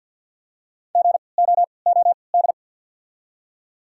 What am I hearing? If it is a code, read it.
Morse code, heard as a keyed tone, about 48 wpm, spaced as for 28 wpm, 700 Hz. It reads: GOOD
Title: 3x Character Spacing - Top 200 Words - 50wpm